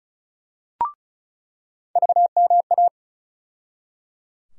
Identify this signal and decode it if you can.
Morse code, heard as a keyed tone, about 34 wpm, 700 Hz. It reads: VMA
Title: USA Call Sign Postfixes - 3 Letters 35wpm